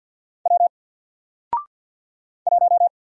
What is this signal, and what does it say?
Morse code, heard as a keyed tone, about 48 wpm, 700 Hz. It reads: W 1
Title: Single Character (Letter:Number:Punctuation:Prosign) Alt 50wpm